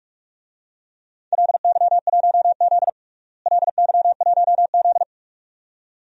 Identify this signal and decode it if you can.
Morse code, heard as a keyed tone, about 46 wpm, 700 Hz. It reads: LY1Z LY1Z
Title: DXCC-100 Call Signs - 4 Characters - Repeated 2x - 45wpm